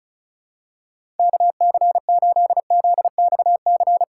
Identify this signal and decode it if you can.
Morse code, heard as a keyed tone, about 34 wpm, 700 Hz. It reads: KC8ZXC
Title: USA Call Signs - 6 Characters - 4x Character Spacing - 35wpm